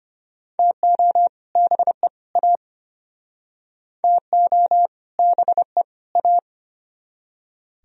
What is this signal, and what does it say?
Morse code, heard as a keyed tone, about 34 wpm, 700 Hz. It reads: TO BE A TO BE A